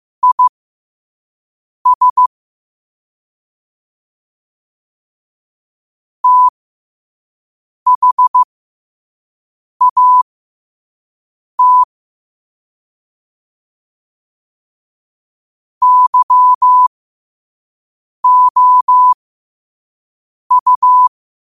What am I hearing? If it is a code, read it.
Morse code, heard as a keyed tone, about 15 wpm, spaced as for 5 wpm, 1000 Hz. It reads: IS THAT YOU
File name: qotdfiles / 5wpm